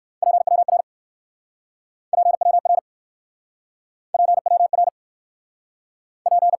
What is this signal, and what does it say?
Morse code, heard as a keyed tone, about 65 wpm, 700 Hz. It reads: PWR PWR PWR P